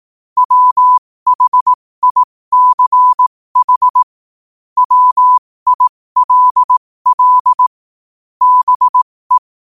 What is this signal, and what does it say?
Morse code, heard as a keyed tone, about 18 wpm, spaced as for 15 wpm, 1000 Hz. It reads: WHICH WILL BE
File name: qotdfiles / 15wpm